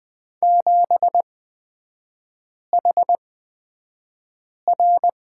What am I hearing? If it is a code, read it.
Morse code, heard as a keyed tone, about 20 wpm, spaced as for 5 wpm, 690 Hz. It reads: 7HR
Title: 229 Lesson 37e - Letter x - Three Characters 20wpm